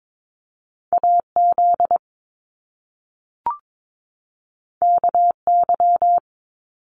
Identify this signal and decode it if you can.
Morse code, heard as a keyed tone, about 22 wpm, 700 Hz. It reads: AZ KY
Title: US States 22wpm